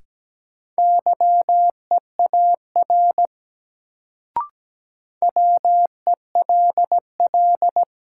Morse code, heard as a keyed tone, about 17 wpm, 700 Hz. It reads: YEAR WELL
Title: Sentences from Top 200 Words - New Word Drill - 17wpm